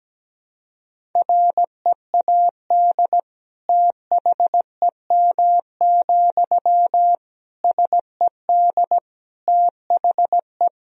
Morse code, heard as a keyed tone, about 17 wpm, 700 Hz. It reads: READ THEM, SED THE
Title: Sentences from Top 400 Words and Abbreviations 17wpm